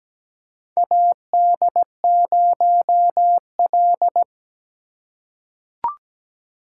Morse code, heard as a keyed tone, about 17 wpm, 700 Hz. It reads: AD0L
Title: USA Call Signs - 4 Characters - 4x Character Spacing - 17wpm